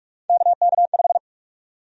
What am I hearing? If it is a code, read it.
Morse code, heard as a keyed tone, about 41 wpm, 700 Hz. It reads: KK5
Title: USA Call Sign Prefixes - 45wpm